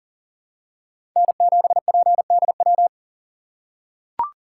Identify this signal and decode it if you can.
Morse code, heard as a keyed tone, about 40 wpm, 700 Hz. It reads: N7PDW
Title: USA Call Signs - 2x Character Spacing - 40wpm